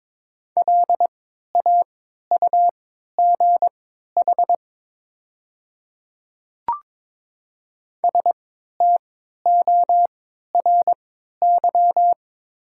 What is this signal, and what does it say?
Morse code, heard as a keyed tone, about 23 wpm, spaced as for 13 wpm, 700 Hz. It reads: LAUGH STORY
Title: 3x Character Spacing - Top 500 Words - 22wpm